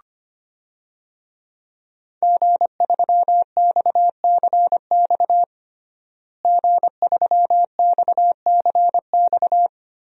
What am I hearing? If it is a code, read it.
Morse code, heard as a keyed tone, about 25 wpm, 700 Hz. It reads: G3XCX G3XCX
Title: DXCC-100 Call Signs - 5 Characters - Repeated 3x - 25wpm